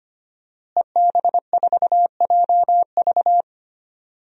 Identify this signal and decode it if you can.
Morse code, heard as a keyed tone, about 25 wpm, 700 Hz. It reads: EB4JV
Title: DXCC-100 Call Signs - 5 Characters - Repeated 3x - 25wpm